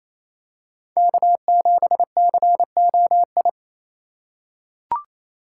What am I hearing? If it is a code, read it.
Morse code, heard as a keyed tone, about 28 wpm, 690 Hz. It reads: K7COI
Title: USA Call Signs - 5 Characters - 4x Character Spacing - 28wpm